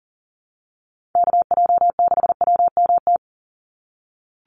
Morse code, heard as a keyed tone, about 40 wpm, 700 Hz. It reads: KJ6WMT
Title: USA Call Signs 40wpm